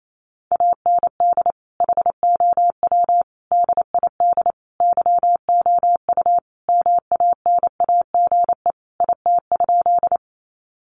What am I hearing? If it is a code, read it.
Morse code, heard as a keyed tone, about 28 wpm, 700 Hz. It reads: AND HOW DID YOU MANAGE IT?
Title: Sentences from Top 1400 Words 28wpm